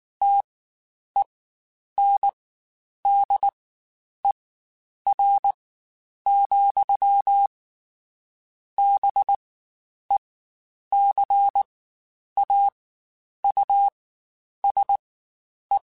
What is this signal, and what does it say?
Morse code, heard as a keyed tone, about 19 wpm, spaced as for 9 wpm, 800 Hz. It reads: TENDER, BECAUSE
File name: Spaced / Chuckles Spaced-mp3s / SoftWords sp @18